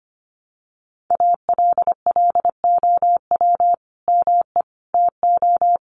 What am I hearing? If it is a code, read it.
Morse code, heard as a keyed tone, about 24 wpm, 700 Hz. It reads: ALLOW ME TO
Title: Sentences from Top 1100 Words 25wpm